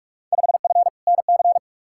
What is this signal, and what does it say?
Morse code, heard as a keyed tone, about 45 wpm, 700 Hz. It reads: 5F NC
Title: Field Day Exchanges Encore - 45wpm